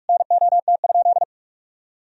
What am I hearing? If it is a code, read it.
Morse code, heard as a keyed tone, about 46 wpm, 700 Hz. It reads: NOT?
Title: Sentences from Top 1000 Words - Review Part 1 - 45wpm